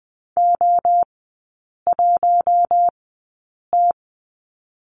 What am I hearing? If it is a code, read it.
Morse code, heard as a keyed tone, about 20 wpm, spaced as for 8 wpm, 700 Hz. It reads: O1T
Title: 067 Lesson 12e - Letter d - Three Characters 20wpm